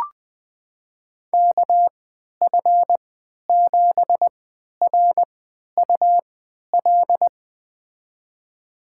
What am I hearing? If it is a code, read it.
Morse code, heard as a keyed tone, about 20 wpm, spaced as for 11 wpm, 700 Hz. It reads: KF7RUL
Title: USA Call Signs - 6 Characters - 3x Character Spacing - 20wpm